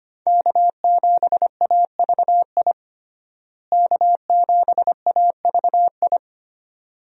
Morse code, heard as a keyed tone, about 25 wpm, 700 Hz. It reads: K7AVI K7AVI
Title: USA Call Signs - 5 Characters - Repeated 2x - 25wpm